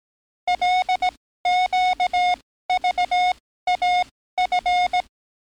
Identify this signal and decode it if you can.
Morse code, heard as a keyed tone, about 18 wpm, spaced as for 14 wpm, 730 Hz. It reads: LQVAF
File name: training / NAVPERS 11049to11060 / NAVPERS 11056 RA RB